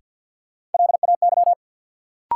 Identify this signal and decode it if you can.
Morse code, heard as a keyed tone, about 47 wpm, 700 Hz. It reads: LAY
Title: Sentences from Top 400 Words - New Word Drill - 50wpm